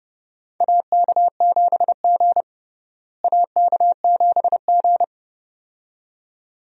Morse code, heard as a keyed tone, about 30 wpm, 700 Hz. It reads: AK7G AK7G
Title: USA Call Signs - 4 Characters - Repeated 2x - 30wpm